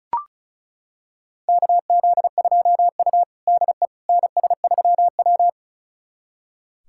Morse code, heard as a keyed tone, about 36 wpm, 700 Hz. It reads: KZ2U DE NS3W
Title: USA Callsign DE Callsign - 4 Characters - 35wpm